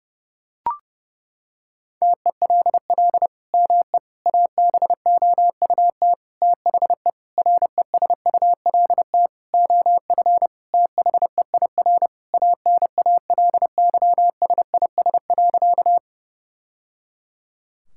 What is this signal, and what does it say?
Morse code, heard as a keyed tone, about 30 wpm, 700 Hz. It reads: TELL ME ABOUT THE RESULT OF THEIR ANALYSIS.
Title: Sentences from Top 1000 Words - Review Part 5 - 30wpm